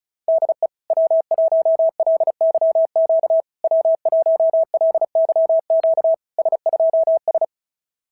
Morse code, heard as a keyed tone, about 33 wpm, 650 Hz. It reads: DE W1LYQ W1LYQ S2S
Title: SOTA QSO Practice 35wpm